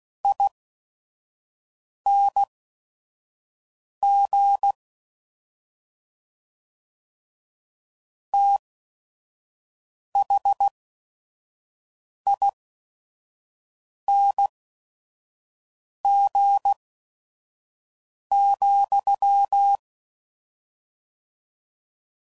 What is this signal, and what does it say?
Morse code, heard as a keyed tone, about 16 wpm, spaced as for 5 wpm, 800 Hz. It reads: ING THING,
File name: Suffixes / Special ING's / Sngl ING's